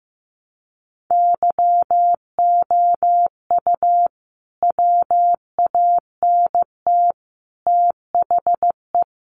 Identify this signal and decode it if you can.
Morse code, heard as a keyed tone, about 15 wpm, 700 Hz. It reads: YOU WANT THE
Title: A Sound Of Thunder 15wpm